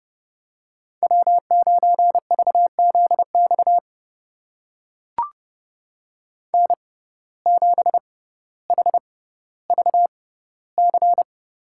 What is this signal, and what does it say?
Morse code, heard as a keyed tone, about 30 wpm, spaced as for 15 wpm, 700 Hz. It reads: W9VZX N7HVC
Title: USA Call Signs - 5 Characters - 6x Character Spacing - 30wpm